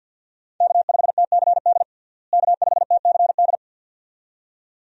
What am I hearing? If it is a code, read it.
Morse code, heard as a keyed tone, about 50 wpm, 700 Hz. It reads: K5TCD K5TCD
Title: USA Call Signs - 5 Characters - Repeated 2x - 50wpm